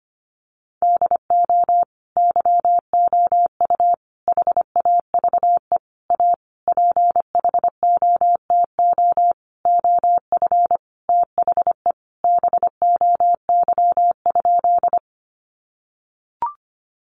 Morse code, heard as a keyed tone, about 25 wpm, 700 Hz. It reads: DO YOU HAVE A PHOTO OF THE BOY?
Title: Sentences from Top 1200 Words 25wpm